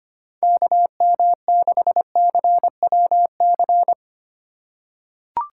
Morse code, heard as a keyed tone, about 25 wpm, 700 Hz. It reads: KM6CWC
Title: USA Call Signs - 6 Characters - 2x Character Spacing - 25wpm